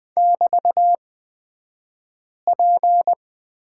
Morse code, heard as a keyed tone, about 20 wpm, spaced as for 5 wpm, 690 Hz. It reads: <BT>P
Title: 228 Lesson 37d - Letter x - Two Characters 20wpm